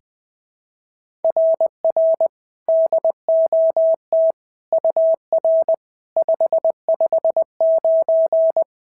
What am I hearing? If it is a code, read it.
Morse code, heard as a keyed tone, about 21 wpm, 660 Hz. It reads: RR DOT UR 559
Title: SOTA QSO Practice 20wpm